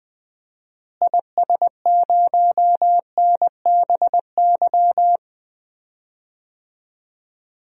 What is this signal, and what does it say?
Morse code, heard as a keyed tone, about 20 wpm, 700 Hz. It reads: IS0NBY
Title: DXCC-100 Call Signs - 6 Characters - Repeated 3x - 20wpm